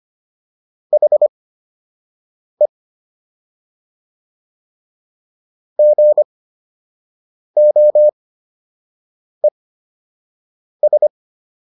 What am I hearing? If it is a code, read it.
Morse code, heard as a keyed tone, about 25 wpm, spaced as for 6 wpm, 600 Hz. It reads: HE GOES